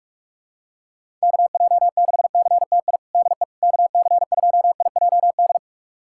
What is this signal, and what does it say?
Morse code, heard as a keyed tone, about 41 wpm, 690 Hz. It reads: KJ6CTI DE KC2IJD
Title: USA Callsign DE Callsign - 6 Characters - 45wpm